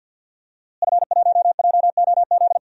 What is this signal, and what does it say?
Morse code, heard as a keyed tone, about 51 wpm, 700 Hz. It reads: F1JOZ
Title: DXCC-100 Call Signs - 5 Characters - Repeated 3x - 50wpm